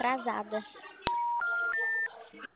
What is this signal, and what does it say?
Special information tone, with speech.